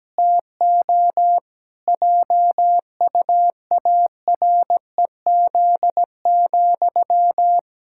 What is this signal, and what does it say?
Morse code, heard as a keyed tone, about 17 wpm, 700 Hz. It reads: TO JUAREZ,
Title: Pangram Challenge v2 - 17wpm